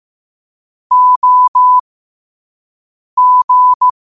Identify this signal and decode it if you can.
Morse code, heard as a keyed tone, about 15 wpm, spaced as for 5 wpm, 1000 Hz. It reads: OG
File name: qotdfiles / 5wpm